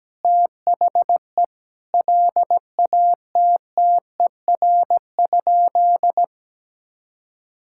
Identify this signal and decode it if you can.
Morse code, heard as a keyed tone, about 17 wpm, 700 Hz. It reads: THE LATTER?
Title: Sentences from Top 1600 Words 17wpm